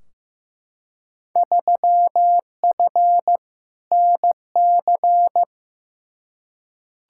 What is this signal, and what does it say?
Morse code, heard as a keyed tone, about 15 wpm, 700 Hz. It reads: 3F NC
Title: Field Day Exchanges Encore - 15wpm